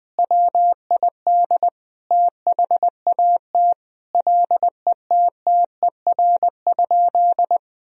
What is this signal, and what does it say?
Morse code, heard as a keyed tone, about 21 wpm, 700 Hz. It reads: WID THAT LETTER?
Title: Sentences from Top 300 Words and Abbreviations 20wpm